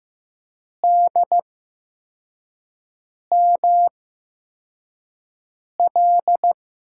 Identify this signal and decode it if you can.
Morse code, heard as a keyed tone, about 15 wpm, spaced as for 4 wpm, 700 Hz. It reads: DML
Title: USA Call Signs - 8x Character Spacing - 15wpm